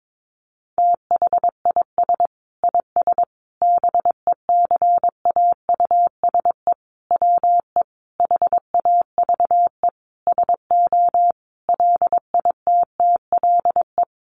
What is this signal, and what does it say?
Morse code, heard as a keyed tone, about 22 wpm, 700 Hz. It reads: THIS IS BECAUSE WE HAVE SO LITTLE
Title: Sentences from Top 1200 Words 22wpm